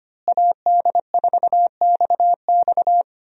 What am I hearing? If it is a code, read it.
Morse code, heard as a keyed tone, about 25 wpm, 700 Hz. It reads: AD4XX
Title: USA Call Signs - 5 Characters - Repeated 3x - 25wpm